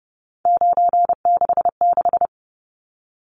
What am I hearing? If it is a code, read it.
Morse code, heard as a keyed tone, about 30 wpm, 700 Hz. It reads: 966
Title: Numbers - 3-Digit 30wpm